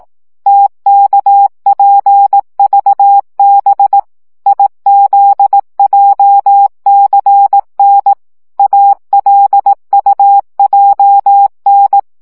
Morse code, heard as a keyed tone, about 18 wpm, 800 Hz. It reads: TKPVB IZJCN ALUJN
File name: morze vizsga